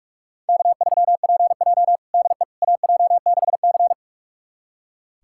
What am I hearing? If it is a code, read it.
Morse code, heard as a keyed tone, about 46 wpm, 700 Hz. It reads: K3PJ DE AJ6C
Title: USA Callsign DE Callsign - 4 Characters - 45wpm